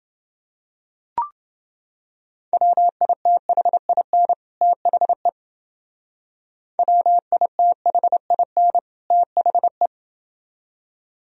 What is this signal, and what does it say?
Morse code, heard as a keyed tone, about 28 wpm, 700 Hz. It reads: WITHIN THE WITHIN THE